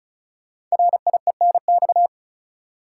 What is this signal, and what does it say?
Morse code, heard as a keyed tone, about 36 wpm, 700 Hz. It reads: RIENX
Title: Code Groups - 5 Letters - 35wpm